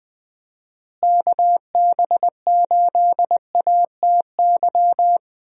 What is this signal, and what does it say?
Morse code, heard as a keyed tone, about 20 wpm, 700 Hz. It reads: KB8ATY
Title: USA Call Signs - 6 Characters - Repeated 2x - 20wpm